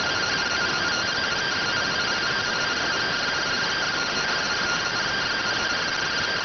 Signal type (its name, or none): two-tone teleprinter signal